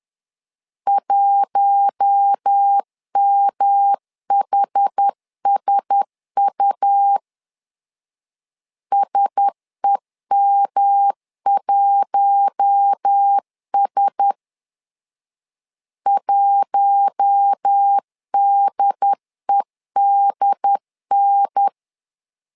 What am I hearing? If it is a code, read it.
Morse code, heard as a keyed tone, about 10 wpm, 800 Hz. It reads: 1MHSU SEM1S 1DEDN
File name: LEZIONI / 4 Q1S 10wpm 2133